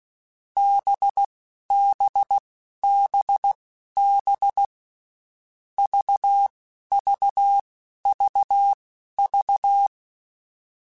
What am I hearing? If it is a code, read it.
Morse code, heard as a keyed tone, about 16 wpm, spaced as for 11 wpm, 800 Hz. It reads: BBBB VVVV
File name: Groups@10 / Groups@10 mp3s / VB46@10